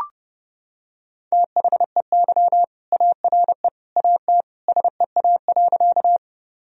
Morse code, heard as a keyed tone, about 30 wpm, 700 Hz. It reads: THEY ARE AT SEA.
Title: Sentences from Top 1000 Words - Review Part 1 - 30wpm